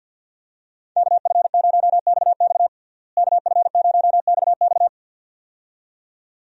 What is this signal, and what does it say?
Morse code, heard as a keyed tone, about 50 wpm, 700 Hz. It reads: KF0XX KF0XX
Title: USA Call Signs - 5 Characters - Repeated 2x - 50wpm